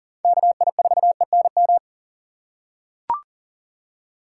Morse code, heard as a keyed tone, about 40 wpm, 700 Hz. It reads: KI4ENM
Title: USA Call Signs - 6 Characters - 6x Character Spacing - 40wpm